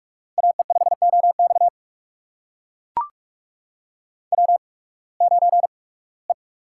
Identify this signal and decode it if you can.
Morse code, heard as a keyed tone, about 44 wpm, spaced as for 23 wpm, 700 Hz. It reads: AE5OX W9E
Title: USA Call Signs - 5 Characters - 8x Character Spacing - 45wpm